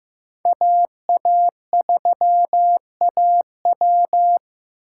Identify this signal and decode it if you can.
Morse code, heard as a keyed tone, about 15 wpm, 700 Hz. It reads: AA3AW
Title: USA Callsign DE Callsign - 5 Characters - 15wpm